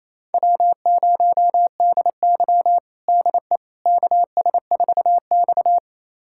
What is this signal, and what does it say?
Morse code, heard as a keyed tone, about 27 wpm, 700 Hz. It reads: W0DY DE KS4X
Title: USA Callsign DE Callsign - 4 Characters - 28wpm